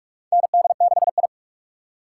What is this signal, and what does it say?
Morse code, heard as a keyed tone, about 41 wpm, 700 Hz. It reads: ND6I
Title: USA Call Signs - 4 Characters - 2x Character Spacing - 45wpm